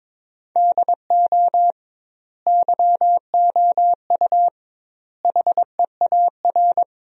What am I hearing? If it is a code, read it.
Morse code, heard as a keyed tone, about 22 wpm, 700 Hz. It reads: DO YOU HEAR
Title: 2x Word-Spacing - Sentences from Top 300 Words - 22wpm